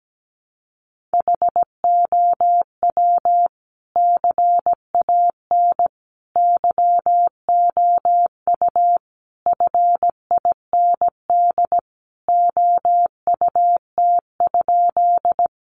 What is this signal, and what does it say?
Morse code, heard as a keyed tone, about 17 wpm, 700 Hz. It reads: HOW CAN YOU FIND OUT?
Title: Sentences from Top 200 Words 17wpm